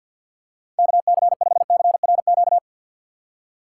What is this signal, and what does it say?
Morse code, heard as a keyed tone, about 48 wpm, 700 Hz. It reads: KC5CRQ
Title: USA Call Signs - 6 Characters - 6x Character Spacing - 50wpm